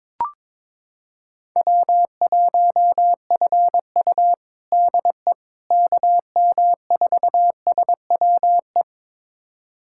Morse code, heard as a keyed tone, about 21 wpm, 700 Hz. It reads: W1FU DE KM4SWE
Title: USA Callsign DE Callsign 22wpm